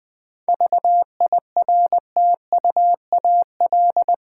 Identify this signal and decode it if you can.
Morse code, heard as a keyed tone, about 20 wpm, 700 Hz. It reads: VIRTUAL
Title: Difficult Words with Hints - 20wpm